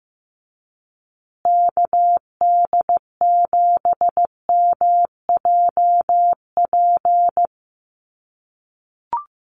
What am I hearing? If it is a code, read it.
Morse code, heard as a keyed tone, about 15 wpm, 700 Hz. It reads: KD7MJP
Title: USA Call Signs Repeated 3x 15wpm